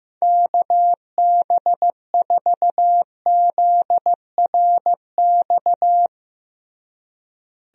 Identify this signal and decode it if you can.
Morse code, heard as a keyed tone, about 15 wpm, 700 Hz. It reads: KB4ZRX
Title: USA Call Signs - 6 Characters - Repeated 2x - 15wpm